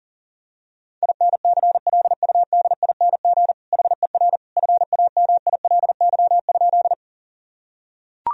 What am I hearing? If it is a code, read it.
Morse code, heard as a keyed tone, about 40 wpm, 700 Hz. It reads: INCLUDING HER FAMILY?
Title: Sentences from Top 1000 Words - Review Part 4 - 40wpm